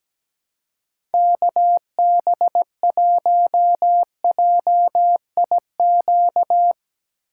Morse code, heard as a keyed tone, about 17 wpm, 700 Hz. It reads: KB1JIQ
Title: USA Call Signs - 6 Characters - 3x Character Spacing - 17wpm